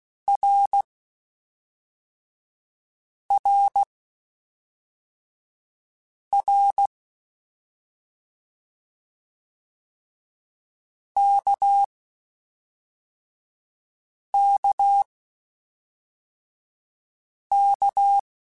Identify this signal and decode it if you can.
Morse code, heard as a keyed tone, about 16 wpm, spaced as for 3 wpm, 800 Hz. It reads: RRR KKK
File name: Short Runs / Short Runs-mp3s / Balanced @5